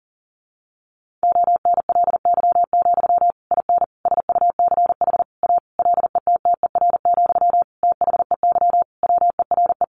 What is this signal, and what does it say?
Morse code, heard as a keyed tone, about 40 wpm, 700 Hz. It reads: ONLY, IN SUCH A LETTER, THEY WERE
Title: Sentences from Top 1500 Words 40wpm